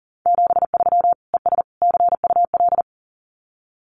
Morse code, heard as a keyed tone, about 40 wpm, 700 Hz. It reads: 73 ES CUL
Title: Common QSO Phrases Encore 40wpm